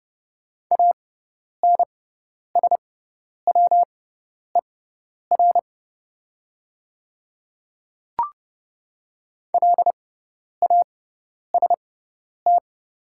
Morse code, heard as a keyed tone, about 31 wpm, spaced as for 10 wpm, 700 Hz. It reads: ANSWER LAST
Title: 6x Character Spacing - Top 300 Words - 30wpm